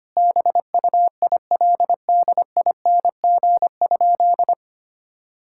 Morse code, heard as a keyed tone, about 25 wpm, 700 Hz. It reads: BUILDING?
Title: Sentences from Top 1000 Words - Review Part 5 - 25wpm